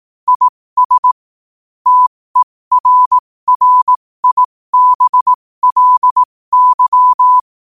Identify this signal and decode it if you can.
Morse code, heard as a keyed tone, about 18 wpm, spaced as for 15 wpm, 1000 Hz. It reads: IS TERRIBLY
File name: qotdfiles / 15wpm